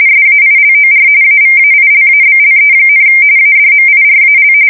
RTTY, about 45.45 baud. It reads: SAILOR AVAILABLE, ADMIRAL
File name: fsk170 / usnavy4